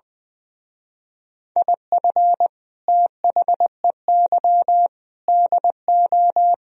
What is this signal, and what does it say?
Morse code, heard as a keyed tone, about 21 wpm, 700 Hz. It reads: IF THEY DO
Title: Speed-Racing - Sets of 4 Words 20wpm